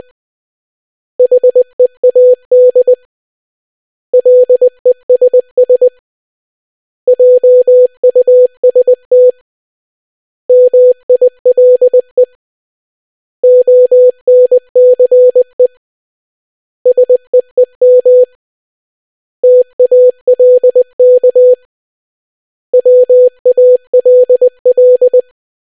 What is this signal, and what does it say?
Morse code, heard as a keyed tone, about 20 wpm, 500 Hz. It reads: HEAD LESS JUST MILE ONCE SEEM TALK WALL